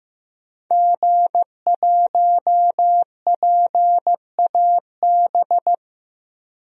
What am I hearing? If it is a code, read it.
Morse code, heard as a keyed tone, about 15 wpm, 700 Hz. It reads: G1PAB
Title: DXCC-100 Call Signs - 5 Characters - 15wpm